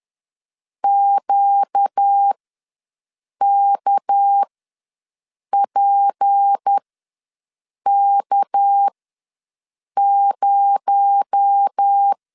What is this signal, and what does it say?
Morse code, heard as a keyed tone, about 10 wpm, spaced as for 6 wpm, 800 Hz. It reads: QKPK0
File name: LEZIONI / 6 PKX0 / 6 PKX0 ws 10wpm 4220